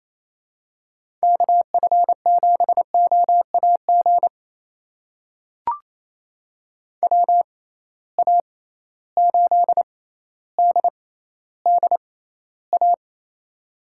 Morse code, heard as a keyed tone, about 28 wpm, spaced as for 14 wpm, 700 Hz. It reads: KF7OAG WA8DDA
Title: USA Call Signs - 6 Characters - 6x Character Spacing - 28wpm